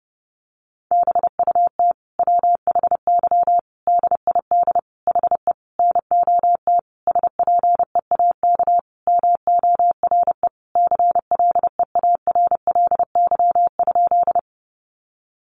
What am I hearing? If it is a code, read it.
Morse code, heard as a keyed tone, about 30 wpm, 700 Hz. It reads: BUT WHY DID HE NOT SPEAK MORE CLEARLY?
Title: Sentences from Top 1300 Words 30wpm